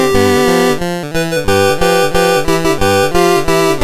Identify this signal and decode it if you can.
There is no code.